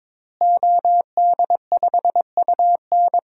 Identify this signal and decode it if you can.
Morse code, heard as a keyed tone, about 22 wpm, 700 Hz. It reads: OD5UN
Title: DXCC-100 Call Signs - 5 Characters - Repeated 3x - 22wpm